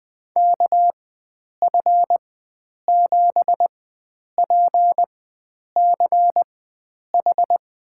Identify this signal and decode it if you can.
Morse code, heard as a keyed tone, about 20 wpm, spaced as for 9 wpm, 700 Hz. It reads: KF7PCH
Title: USA Call Signs - 6 Characters - 4x Character Spacing - 20wpm